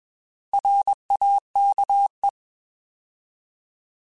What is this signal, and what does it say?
Morse code, heard as a keyed tone, about 21 wpm, 800 Hz. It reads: RAKE